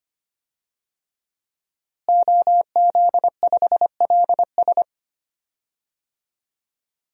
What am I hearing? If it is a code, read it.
Morse code, heard as a keyed tone, about 25 wpm, 700 Hz. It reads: OZ5LS
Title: DXCC-100 Call Signs - 5 Characters - Repeated 3x - 25wpm